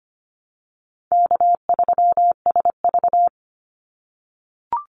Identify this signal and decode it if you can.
Morse code, heard as a keyed tone, about 25 wpm, 700 Hz. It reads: K3SV
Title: Easy Call Signs - 25wpm